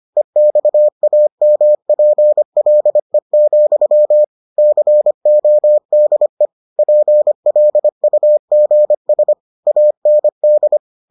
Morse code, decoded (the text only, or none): EXAMPLE, CODE PLUGS AND